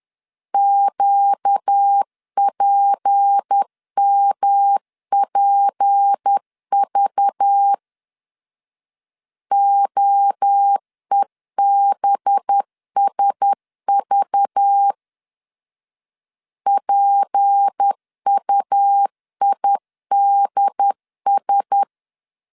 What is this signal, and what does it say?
Morse code, heard as a keyed tone, about 10 wpm, 800 Hz. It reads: QPMPV OEBSV PUIDS
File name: LEZIONI / 6 PKX0 / 6 P 10wpm 2432